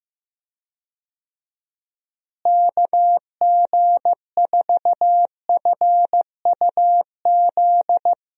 Morse code, heard as a keyed tone, about 15 wpm, 700 Hz. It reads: KG4FUZ
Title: USA Call Signs - 6 Characters - Repeated 2x - 15wpm